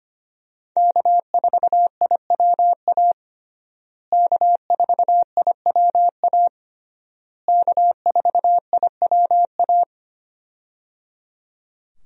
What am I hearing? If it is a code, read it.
Morse code, heard as a keyed tone, about 25 wpm, 700 Hz. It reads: K4IWA K4IWA K4IWA
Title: USA Call Signs - 5 Characters - Repeated 3x - 25wpm